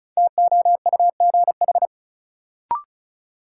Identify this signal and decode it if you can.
Morse code, heard as a keyed tone, about 36 wpm, 700 Hz. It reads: TOUGH
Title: Sentences from Top 1200 Words - New Word Drill Alt - 35wpm